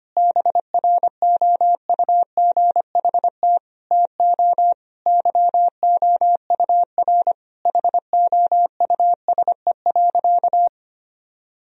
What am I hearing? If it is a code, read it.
Morse code, heard as a keyed tone, about 25 wpm, 700 Hz. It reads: BROUGHT TO YOUR HOUSE.
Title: Sentences from Top 1000 Words - Review Part 2 - 25wpm